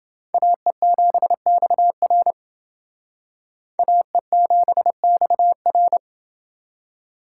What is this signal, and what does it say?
Morse code, heard as a keyed tone, about 28 wpm, 700 Hz. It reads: AE7XR AE7XR